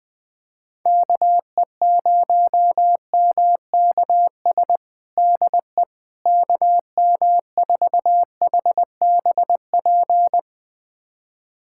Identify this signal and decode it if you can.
Morse code, heard as a keyed tone, about 20 wpm, 700 Hz. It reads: KE0MKS DE KM4HBP
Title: USA Callsign DE Callsign - 6 Characters - 20wpm